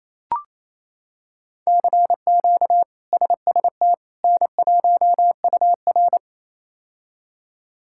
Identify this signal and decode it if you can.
Morse code, heard as a keyed tone, about 28 wpm, 700 Hz. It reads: CQ SST N1UR
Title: K1USN SST QSO Practice - 28wpm